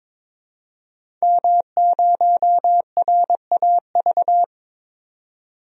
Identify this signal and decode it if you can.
Morse code, heard as a keyed tone, about 21 wpm, 700 Hz. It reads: M0RAV
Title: DXCC-100 Call Signs - 5 Characters - 22wpm